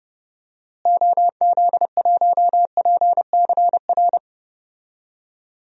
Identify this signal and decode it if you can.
Morse code, heard as a keyed tone, about 30 wpm, 700 Hz. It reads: OZ1PCR
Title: DXCC-100 Call Signs - 6 Characters - Repeated 2x - 30wpm